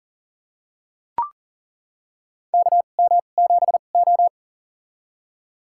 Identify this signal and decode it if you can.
Morse code, heard as a keyed tone, about 40 wpm, spaced as for 29 wpm, 700 Hz. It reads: KM7O
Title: USA Call Signs - 4 Characters - 2x Character Spacing - 40wpm